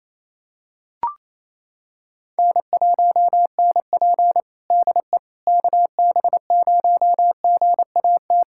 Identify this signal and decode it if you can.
Morse code, heard as a keyed tone, about 28 wpm, 700 Hz. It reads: N1NP DE KB0GAT
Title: USA Callsign DE Callsign 28wpm